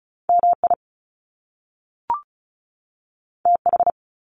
Morse code, heard as a keyed tone, about 36 wpm, 700 Hz. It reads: MI TH
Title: Top Two Letter Combinations in Words 35wpm